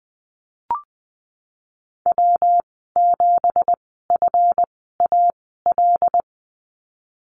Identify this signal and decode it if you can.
Morse code, heard as a keyed tone, about 20 wpm, spaced as for 15 wpm, 700 Hz. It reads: W7FAL
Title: 231 Lesson 37g - Letter x - Call Signs 20wpm